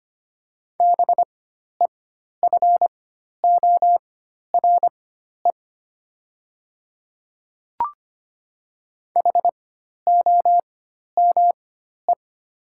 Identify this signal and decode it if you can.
Morse code, heard as a keyed tone, about 25 wpm, spaced as for 12 wpm, 700 Hz. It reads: BEFORE HOME
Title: 4x Character Spacing - Top 200 Words - 25wpm